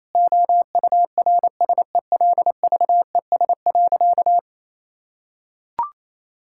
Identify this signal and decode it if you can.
Morse code, heard as a keyed tone, about 28 wpm, 700 Hz. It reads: OURSELVES.
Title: Sentences from Top 1600 Words 28wpm